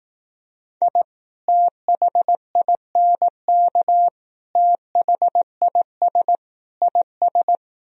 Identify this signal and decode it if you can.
Morse code, heard as a keyed tone, about 18 wpm, 700 Hz. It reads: I THINK THIS IS
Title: Speed-Racing - Sets of 4 Words 15wpm